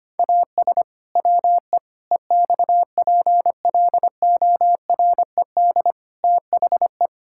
Morse code, heard as a keyed tone, about 25 wpm, 700 Hz. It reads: AS WE EXPLORED THE
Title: Pangram Challenge v2 - 25wpm